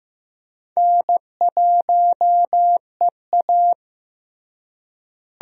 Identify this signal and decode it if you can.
Morse code, heard as a keyed tone, about 15 wpm, 700 Hz. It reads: N1EA
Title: USA Call Signs - 4 Characters - Repeated 2x - 15wpm